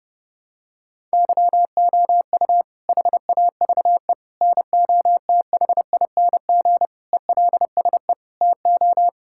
Morse code, heard as a keyed tone, about 30 wpm, 700 Hz. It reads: YOU HAVE NOTHING ELSE TO
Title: Sentences from Top 1000 Words - Review Part 6 - 30wpm